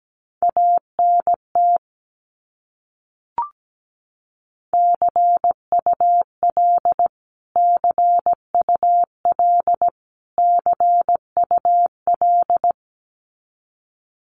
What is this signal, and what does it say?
Morse code, heard as a keyed tone, about 17 wpm, 700 Hz. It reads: ANT CUL CUL CUL
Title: Abbreviations Repeated 3x - 17wpm